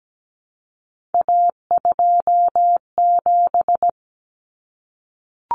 Morse code, heard as a keyed tone, about 17 wpm, 700 Hz. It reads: A27
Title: Common Cut Numbers - 3-Digit 17wpm